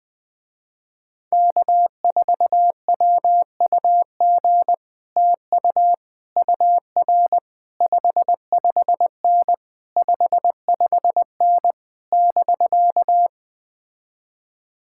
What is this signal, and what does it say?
Morse code, heard as a keyed tone, about 20 wpm, 700 Hz. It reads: K4WUG TU UR 55N 55N <BK>
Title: Speed-Racing Common QSO Phrases - 17wpm